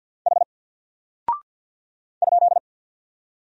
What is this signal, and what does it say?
Morse code, heard as a keyed tone, about 48 wpm, 700 Hz. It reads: H ?
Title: Single Character (Letter:Number:Punctuation:Prosign) Alt 50wpm